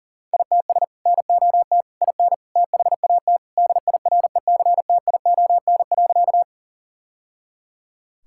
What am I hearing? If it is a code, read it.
Morse code, heard as a keyed tone, about 37 wpm, 700 Hz. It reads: ITS NOT IN THAT DIRECTION.
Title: Sentences from Top 1000 Words - Review Part 5 - 40wpm